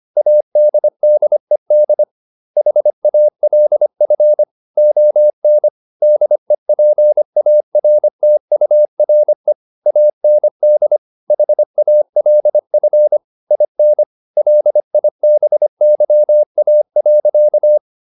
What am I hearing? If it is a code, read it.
Morse code, heard as a keyed tone, about 25 wpm, 600 Hz. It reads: ADDED HALF ON DEPARTURE AND HALF IN LIBYA.